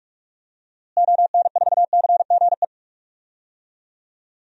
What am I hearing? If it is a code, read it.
Morse code, heard as a keyed tone, about 46 wpm, 700 Hz. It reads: ON4CGE
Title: DXCC-100 Call Signs - 6 Characters - Repeated 2x - 45wpm